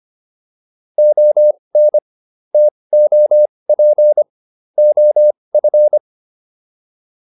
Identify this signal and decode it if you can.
Morse code, heard as a keyed tone, about 25 wpm, spaced as for 20 wpm, 600 Hz. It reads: ON TOP OF